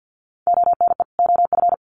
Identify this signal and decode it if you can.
Morse code, heard as a keyed tone, about 50 wpm, 700 Hz. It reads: ONE OF
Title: Sets of 2 Words 50wpm